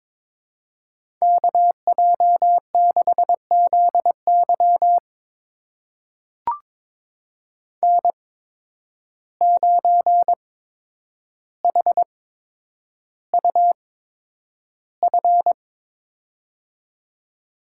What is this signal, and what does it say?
Morse code, heard as a keyed tone, about 22 wpm, spaced as for 9 wpm, 700 Hz. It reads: KJ6ZY N9HUF
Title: USA Call Signs - 5 Characters - 8x Character Spacing - 22wpm